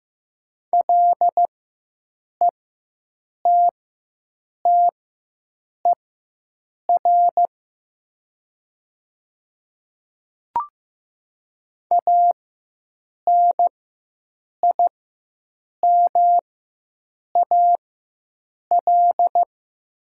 Morse code, heard as a keyed tone, about 15 wpm, spaced as for 7 wpm, 700 Hz. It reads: LETTER ANIMAL
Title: 4x Character Spacing - Top 300 Words - 15wpm